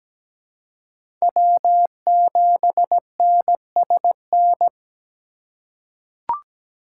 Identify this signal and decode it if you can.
Morse code, heard as a keyed tone, about 17 wpm, 700 Hz. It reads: W7NSN
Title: USA Call Signs - 5 Characters - 4x Character Spacing - 17wpm